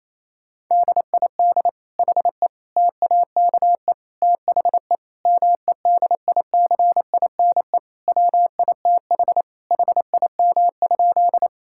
Morse code, heard as a keyed tone, about 28 wpm, 700 Hz. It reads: DID HE TAKE THE MEDICINE WITH HIM?
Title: Sentences from Top 1000 Words - Review Part 6 - 28wpm